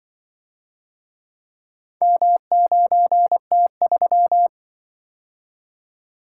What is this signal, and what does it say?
Morse code, heard as a keyed tone, about 26 wpm, 700 Hz. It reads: M9T3